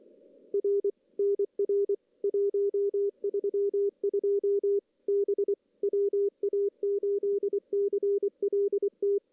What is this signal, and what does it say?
Morse code, heard as a keyed tone, about 25 wpm, 400 Hz. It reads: R NR 132 B WA8CLT
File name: CW / CW-SS-k5tr- -1700